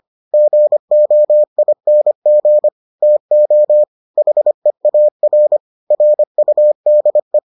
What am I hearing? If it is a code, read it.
Morse code, heard as a keyed tone, about 25 wpm, 600 Hz. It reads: GOING TO HEAR RUDE